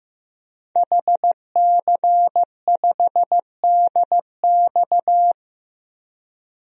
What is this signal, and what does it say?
Morse code, heard as a keyed tone, about 15 wpm, 700 Hz. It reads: HC5DX
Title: DXCC-100 Call Signs - 5 Characters - Repeated 3x - 15wpm